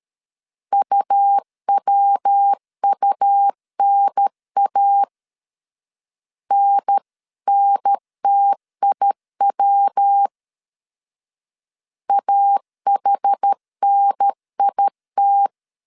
Morse code, heard as a keyed tone, about 13 wpm, 800 Hz. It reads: UWUNA NNTIW AHNIT
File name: CW / DET / 1 3 1500C S B 12wpm 2828